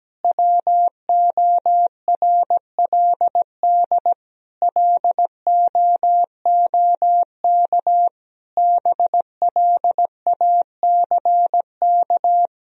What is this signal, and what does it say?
Morse code, heard as a keyed tone, about 17 wpm, 700 Hz. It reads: WORLD LOOK BLACK
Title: Sentences from Top 1000 Words - Review Part 2 - 17wpm